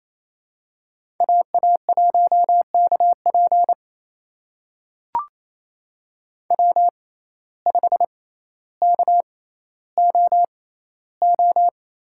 Morse code, heard as a keyed tone, about 28 wpm, spaced as for 14 wpm, 700 Hz. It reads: AA1KP W5KOO
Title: USA Call Signs - 6x Character Spacing - 28wpm